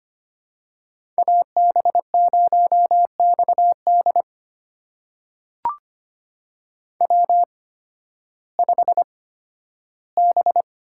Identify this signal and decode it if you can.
Morse code, heard as a keyed tone, about 25 wpm, spaced as for 13 wpm, 700 Hz. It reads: AB0XD W5B
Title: USA Call Signs - 5 Characters - 8x Character Spacing - 25wpm